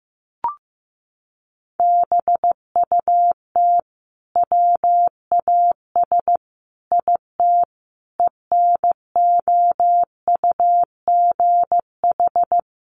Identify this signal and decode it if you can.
Morse code, heard as a keyed tone, about 15 wpm, 700 Hz. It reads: BUT WAS IT ENOUGH
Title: Sentences from Top 1400 Words 15wpm